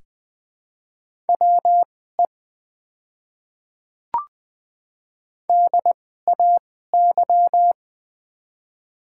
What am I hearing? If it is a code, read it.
Morse code, heard as a keyed tone, about 20 wpm, spaced as for 14 wpm, 700 Hz. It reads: WE DAY
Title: 2x Character Spacing - Top 100 Words - 20wpm